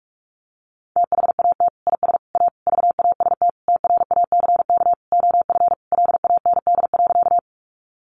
Morse code, heard as a keyed tone, about 44 wpm, 700 Hz. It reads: THAT IS A VAST TRACK OF LAND.
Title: Sentences from Top 1300 Words 45wpm